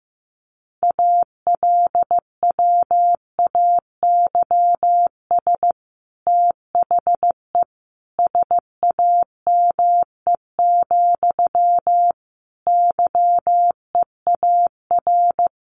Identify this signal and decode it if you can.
Morse code, heard as a keyed tone, about 15 wpm, 700 Hz. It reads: ALWAYS THE SAME, YEAR
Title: Sentences from Top 300 Words 15wpm